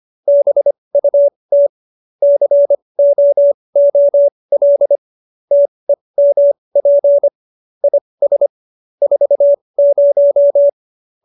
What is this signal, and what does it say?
Morse code, heard as a keyed tone, about 25 wpm, spaced as for 20 wpm, 570 Hz. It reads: BUT COOL TEMP IS 40
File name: QSO302 20